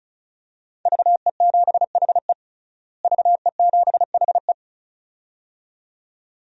Morse code, heard as a keyed tone, about 36 wpm, 700 Hz. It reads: VE7HE VE7HE
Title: DXCC-100 Call Signs - 5 Characters - Repeated 2x - 35wpm